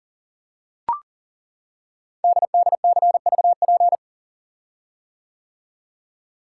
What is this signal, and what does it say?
Morse code, heard as a keyed tone, about 39 wpm, 700 Hz. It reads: DDCVP
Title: Code Groups - 5 Letters - 40wpm